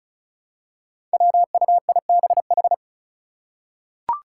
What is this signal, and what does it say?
Morse code, heard as a keyed tone, about 34 wpm, 700 Hz. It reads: WUIBH
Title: Five Challenging Letter-Number Combinations 35wpm